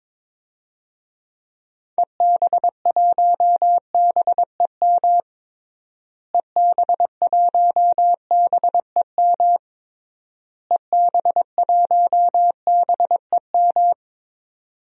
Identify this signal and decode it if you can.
Morse code, heard as a keyed tone, about 21 wpm, 700 Hz. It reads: EB1BEM EB1BEM EB1BEM
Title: DXCC-100 Call Signs - 6 Characters - Repeated 3x - 22wpm